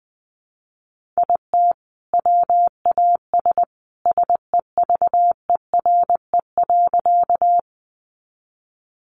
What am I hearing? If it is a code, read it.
Morse code, heard as a keyed tone, about 20 wpm, 700 Hz. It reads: IT WAS SEVERE.
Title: Sentences from Top 1400 Words 20wpm